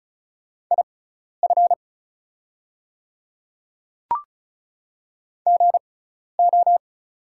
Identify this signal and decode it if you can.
Morse code, heard as a keyed tone, about 36 wpm, spaced as for 12 wpm, 700 Hz. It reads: IF GO
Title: 6x Character Spacing - Top 100 Words - 35wpm